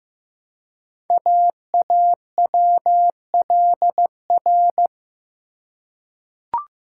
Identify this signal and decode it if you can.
Morse code, heard as a keyed tone, about 15 wpm, 700 Hz. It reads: AAWLR
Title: Code Groups - 5 Letters - 15wpm